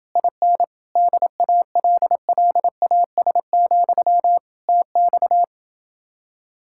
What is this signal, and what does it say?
Morse code, heard as a keyed tone, about 28 wpm, 700 Hz. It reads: IN DALLAS, TX